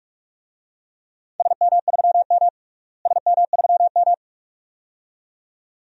Morse code, heard as a keyed tone, about 44 wpm, 700 Hz. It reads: SM3M SM3M
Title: DXCC-100 Call Signs - 4 Characters - Repeated 2x - 45wpm